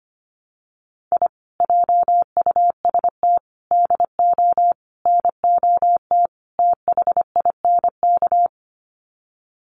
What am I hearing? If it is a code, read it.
Morse code, heard as a keyed tone, about 25 wpm, 700 Hz. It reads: I JUST DO NOT THINK
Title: Sets of 5 Words Encore 25wpm